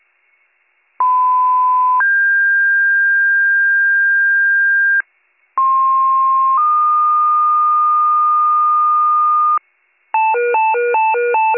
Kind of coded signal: two-tone page

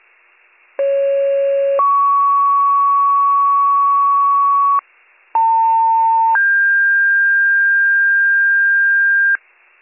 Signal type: two-tone page

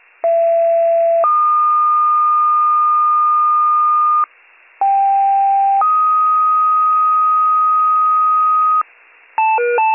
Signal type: two-tone page

